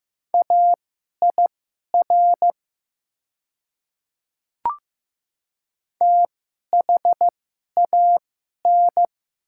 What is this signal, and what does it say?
Morse code, heard as a keyed tone, about 15 wpm, spaced as for 11 wpm, 700 Hz. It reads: AIR THAN
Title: 2x Character Spacing - Top 200 Words - 15wpm